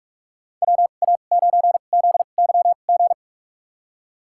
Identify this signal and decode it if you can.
Morse code, heard as a keyed tone, about 44 wpm, spaced as for 32 wpm, 700 Hz. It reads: WA9ZYG
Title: USA Call Signs - 2x Character Spacing - 45wpm